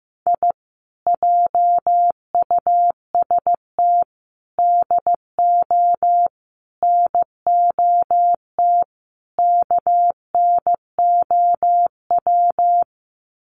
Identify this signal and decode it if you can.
Morse code, heard as a keyed tone, about 15 wpm, 700 Hz. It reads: I JUST DO NOT KNOW
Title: Sets of 5 Words Encore 15wpm